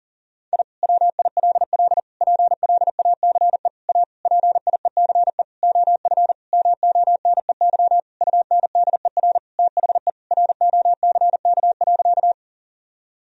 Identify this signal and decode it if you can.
Morse code, heard as a keyed tone, about 40 wpm, 700 Hz. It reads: I WILL PLACE A PIECE OF MONEY UNDER THE ROCK.
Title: Sentences from Top 1000 Words - Review Part 2 - 40wpm